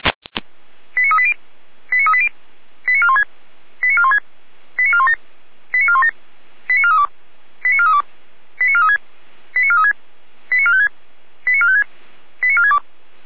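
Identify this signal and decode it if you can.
Five-tone selective call (ZVEI), heard as quick runs of five tones, tones of about 70 ms. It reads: 89380 89380 89416 89416 89417 89417 89432 89432 89436 89436 89456 89456 89462
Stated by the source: ZVEI five-tone selective call received by SDR.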